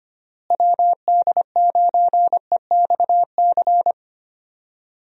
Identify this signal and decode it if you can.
Morse code, heard as a keyed tone, about 25 wpm, 700 Hz. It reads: WD9EXC
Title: USA Call Signs - 6 Characters - Repeated 3x - 25wpm